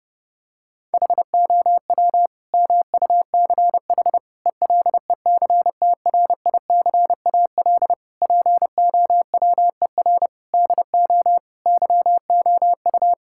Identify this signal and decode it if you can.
Morse code, heard as a keyed tone, about 30 wpm, 700 Hz. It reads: HOW MUCH ELECTRICAL POWER DO YOU
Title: Sentences from Top 1000 Words - Review Part 6 - 30wpm